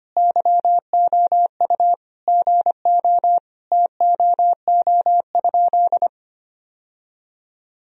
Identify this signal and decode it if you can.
Morse code, heard as a keyed tone, about 25 wpm, 700 Hz. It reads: YOU GO TOO?
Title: Sentences from Top 1000 Words - Review Part 2 - 25wpm